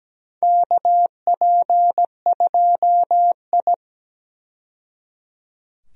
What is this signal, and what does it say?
Morse code, heard as a keyed tone, about 17 wpm, 700 Hz. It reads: KP2I
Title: DXCC-100 Call Signs - 4 Characters - 17wpm